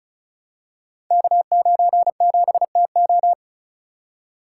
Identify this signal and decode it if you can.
Morse code, heard as a keyed tone, about 35 wpm, 700 Hz. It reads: K97TO
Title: Code Groups - 5 Letter:Numbers - 35wpm